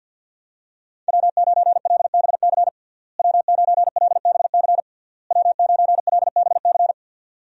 Morse code, heard as a keyed tone, about 49 wpm, 700 Hz. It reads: W9LBC W9LBC W9LBC
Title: USA Call Signs - 5 Characters - Repeated 3x - 50wpm